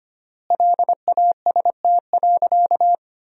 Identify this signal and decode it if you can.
Morse code, heard as a keyed tone, about 25 wpm, 700 Hz. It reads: LAST.
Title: 3x Word-Spacing - Sentences from Top 300 Words - 25wpm